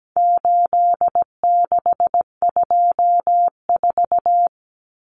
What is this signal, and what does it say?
Morse code, heard as a keyed tone, about 17 wpm, 700 Hz. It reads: 8624
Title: Common Cut Numbers - 4-Digit 17wpm